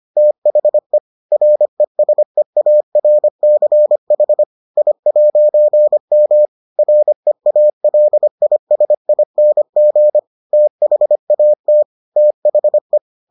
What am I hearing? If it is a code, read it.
Morse code, heard as a keyed tone, about 25 wpm, 600 Hz. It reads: THE RESEARCH I'M REALISING THAT THE